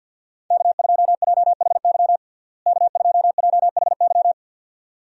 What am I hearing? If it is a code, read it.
Morse code, heard as a keyed tone, about 51 wpm, 700 Hz. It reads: K2JHY K2JHY
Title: USA Call Signs - 5 Characters - Repeated 2x - 50wpm